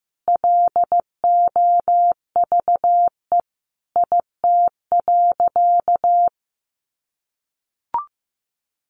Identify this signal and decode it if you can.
Morse code, heard as a keyed tone, about 15 wpm, 700 Hz. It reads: LOVE IT.
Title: Sentences from Top 1500 Words 15wpm